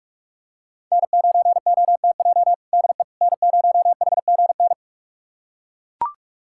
Morse code, heard as a keyed tone, about 41 wpm, 700 Hz. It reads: N9OTJ DE N0HGN
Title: USA Callsign DE Callsign - 5 Characters - 45wpm